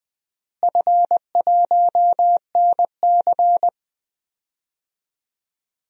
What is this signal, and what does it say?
Morse code, heard as a keyed tone, about 20 wpm, 700 Hz. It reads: F1NC
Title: DXCC-100 Call Signs - 4 Characters - Repeated 2x - 20wpm